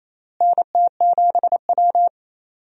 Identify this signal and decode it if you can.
Morse code, heard as a keyed tone, about 28 wpm, 700 Hz. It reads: NT7W
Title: USA Call Signs - 4 Characters - Repeated 2x - 28wpm